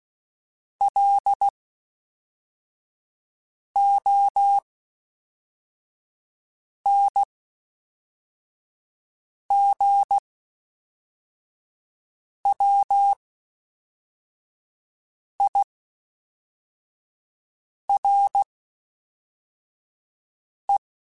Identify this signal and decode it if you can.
Morse code, heard as a keyed tone, about 16 wpm, spaced as for 4 wpm, 800 Hz. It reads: LONGWIRE
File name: (spaced) / FromSMs spaced mp3s / Classes@5